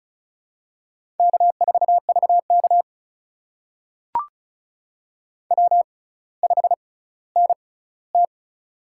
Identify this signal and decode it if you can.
Morse code, heard as a keyed tone, about 36 wpm, spaced as for 18 wpm, 700 Hz. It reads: K4VK W5NT
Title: USA Call Signs - 4 Characters - 6x Character Spacing - 35wpm